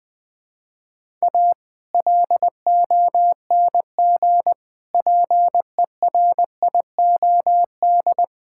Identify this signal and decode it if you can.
Morse code, heard as a keyed tone, about 21 wpm, 700 Hz. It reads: A LONG PERIOD
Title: Build A Phrase - 5-Word Phrases - 20wpm